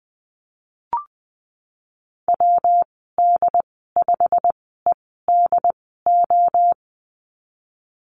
Morse code, heard as a keyed tone, about 20 wpm, spaced as for 14 wpm, 700 Hz. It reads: WD5EDO
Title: 219 Lesson 35g - Number 0 - Call Signs 20wpm